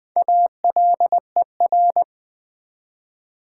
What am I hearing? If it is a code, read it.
Morse code, heard as a keyed tone, about 20 wpm, 700 Hz. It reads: ALER
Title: Build A Word - 1 Letter At A Time - 5-Letter Words - 20wpm